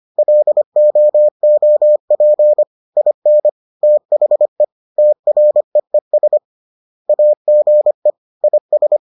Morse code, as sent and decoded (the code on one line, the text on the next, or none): .-.. --- --- .--. / .. -. / - .... / - .-. . ... / .- --. / .. ...
LOOP IN THE TREES AGE IS